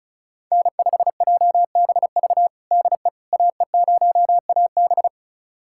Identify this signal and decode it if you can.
Morse code, heard as a keyed tone, about 36 wpm, 700 Hz. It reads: N5JBV DE AE0AB
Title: USA Callsign DE Callsign - 5 Characters - 35wpm